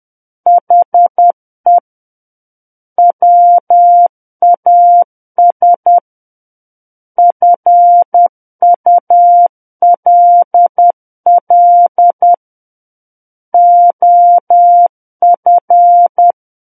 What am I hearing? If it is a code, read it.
Morse code, heard as a keyed tone, about 10 wpm, 700 Hz. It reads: HE WAS FULL OF
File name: Book One / 01-Chapter 1 - 10 WPM